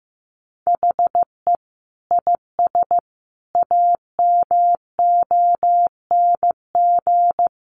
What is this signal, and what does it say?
Morse code, heard as a keyed tone, about 15 wpm, 700 Hz. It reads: HE IS AMONG
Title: Sentences from Top 500 Words 15wpm